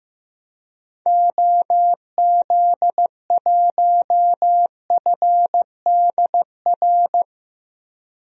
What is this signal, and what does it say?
Morse code, heard as a keyed tone, about 15 wpm, 700 Hz. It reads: OZ1FDR
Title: DXCC-100 Call Signs - 6 Characters - Repeated 2x - 15wpm